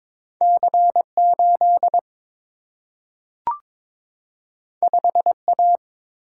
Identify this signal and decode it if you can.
Morse code, heard as a keyed tone, about 22 wpm, 700 Hz. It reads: C8 5A
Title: Single-Letter Single-Number 22wpm